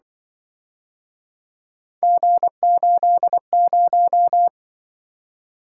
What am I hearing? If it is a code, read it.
Morse code, heard as a keyed tone, about 25 wpm, 700 Hz. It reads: G80